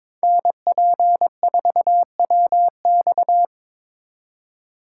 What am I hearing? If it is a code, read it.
Morse code, heard as a keyed tone, about 26 wpm, 700 Hz. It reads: NP4WX
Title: Speed Racing - USA 2x2 Call Signs 22wpm